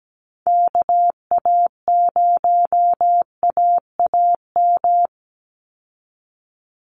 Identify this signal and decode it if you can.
Morse code, heard as a keyed tone, about 17 wpm, 700 Hz. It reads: KA0AAM
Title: Call Signs Repeated 3x - 17wpm